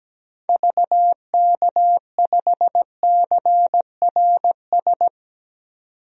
Morse code, heard as a keyed tone, about 17 wpm, 700 Hz. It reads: VK5CRS
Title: DXCC-100 Call Signs - 6 Characters - 17wpm